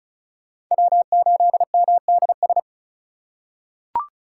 Morse code, heard as a keyed tone, about 34 wpm, 700 Hz. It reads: W8MDS
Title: USA Call Signs - 5 Characters - 3x Character Spacing - 35wpm